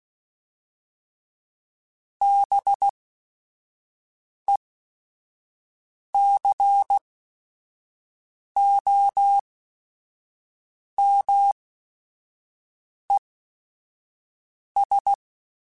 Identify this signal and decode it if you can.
Morse code, heard as a keyed tone, about 16 wpm, spaced as for 5 wpm, 800 Hz. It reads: BECOMES
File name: Technical Subjects / Technical Subjects-mp3s / AND Gate @5